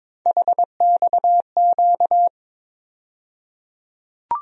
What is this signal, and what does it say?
Morse code, heard as a keyed tone, about 22 wpm, 700 Hz. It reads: HXQ
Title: USA Call Sign Postfixes - 3 Letters 22wpm